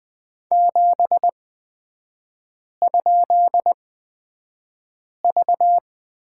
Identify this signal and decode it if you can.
Morse code, heard as a keyed tone, about 20 wpm, spaced as for 5 wpm, 700 Hz. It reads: 7?V
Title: 224 Lesson 36e - BT - Three Characters 20wpm